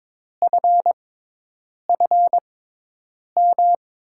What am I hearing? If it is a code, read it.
Morse code, heard as a keyed tone, about 22 wpm, spaced as for 8 wpm, 700 Hz. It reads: FFM
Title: USA Call Signs - 6 Characters - 6x Character Spacing - 22wpm